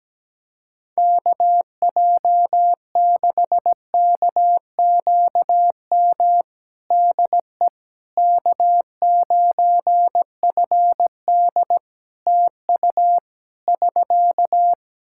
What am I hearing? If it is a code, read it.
Morse code, heard as a keyed tone, about 17 wpm, 700 Hz. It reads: KJ6KQM DE K9FD TU <SK>
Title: K1USN SST QSO Practice 17wpm